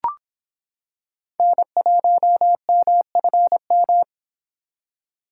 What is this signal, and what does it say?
Morse code, heard as a keyed tone, about 26 wpm, 700 Hz. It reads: N1MFM